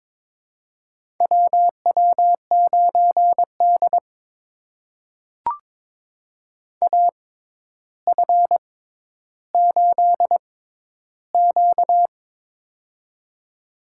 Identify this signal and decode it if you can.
Morse code, heard as a keyed tone, about 22 wpm, spaced as for 11 wpm, 700 Hz. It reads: WW9D AF8Q
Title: USA Call Signs - 4 Characters - 6x Character Spacing - 22wpm